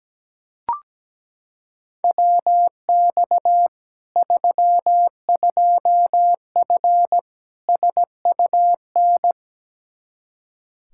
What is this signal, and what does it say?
Morse code, heard as a keyed tone, about 17 wpm, 700 Hz. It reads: WX 32F SUN
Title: Practice QSO - Exchange 2 - Info - 17wpm